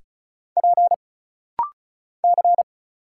Morse code, heard as a keyed tone, about 34 wpm, 700 Hz. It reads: P C
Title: Single Letters - Rapid-Fire 35wpm